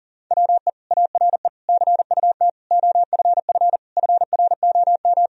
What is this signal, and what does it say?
Morse code, heard as a keyed tone, about 40 wpm, 700 Hz. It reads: WE ARE CUT OFF FROM
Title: Sentences from Top 1000 Words - Review Part 1 - 40wpm